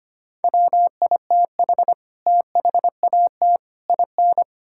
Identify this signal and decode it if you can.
Morse code, heard as a keyed tone, about 25 wpm, 700 Hz. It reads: WITH THAT IN
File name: Sets of 9 Words - 25wpm